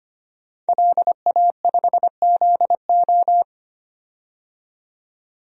Morse code, heard as a keyed tone, about 25 wpm, 700 Hz. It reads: LA5ZO
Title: DXCC-100 Call Signs - 5 Characters - 25wpm